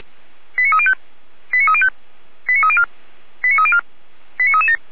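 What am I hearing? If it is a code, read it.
Five-tone selective call (ZVEI), heard as quick runs of five tones, tones of about 80 ms. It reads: 89305 89305 89304 89304 89337
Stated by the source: ZVEI five-tone fire brigade alerting call received on an SDR.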